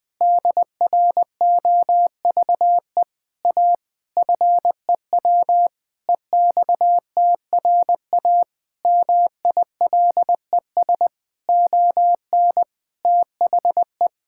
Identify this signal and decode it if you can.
Morse code, heard as a keyed tone, about 20 wpm, 700 Hz. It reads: DROVE A FEW EXTRA MILES ON THE
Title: Pangram Challenge v2 - 20wpm